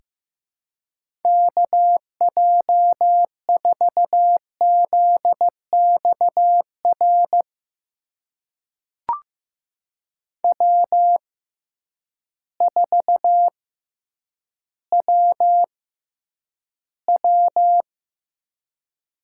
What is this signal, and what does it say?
Morse code, heard as a keyed tone, about 16 wpm, spaced as for 9 wpm, 700 Hz. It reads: KJ4ZXR W4WW
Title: USA Call Signs - 6x Character Spacing - 15wpm